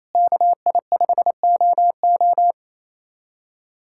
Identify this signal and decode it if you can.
Morse code, heard as a keyed tone, about 28 wpm, 700 Hz. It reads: KI5OO
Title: USA Callsign DE Callsign 28wpm